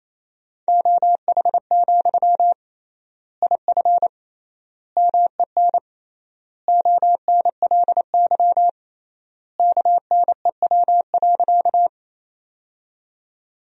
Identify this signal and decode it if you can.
Morse code, heard as a keyed tone, about 27 wpm, 700 Hz. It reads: OH, IF MEN ONLY KNEW.
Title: 3x Word-Spacing - Sentences from Top 500 Words - 28wpm